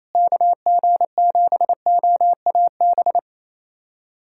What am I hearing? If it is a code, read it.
Morse code, heard as a keyed tone, about 28 wpm, 700 Hz. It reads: KG7OAB
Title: USA Callsign DE Callsign - 6 Characters - 28wpm